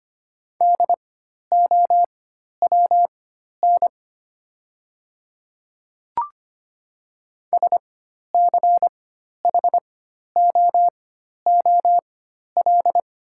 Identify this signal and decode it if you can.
Morse code, heard as a keyed tone, about 25 wpm, spaced as for 12 wpm, 700 Hz. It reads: DOWN SCHOOL
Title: 4x Character Spacing - Top 300 Words - 25wpm